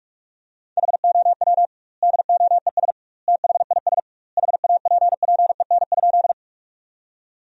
Morse code, heard as a keyed tone, about 44 wpm, 700 Hz. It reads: HOW DOES THIS HAPPEN?
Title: 2x Word-Spacing - Sentences from Top 400 Words - 45wpm